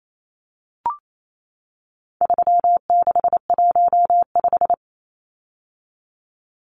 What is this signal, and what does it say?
Morse code, heard as a keyed tone, about 28 wpm, 700 Hz. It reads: 3615
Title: Numbers - 4-Digit 28wpm